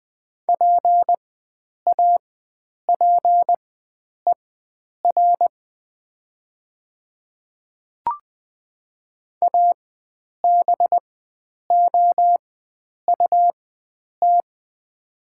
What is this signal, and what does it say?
Morse code, heard as a keyed tone, about 20 wpm, spaced as for 9 wpm, 690 Hz. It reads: PAPER ABOUT
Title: 4x Character Spacing - Top 300 Words - 20wpm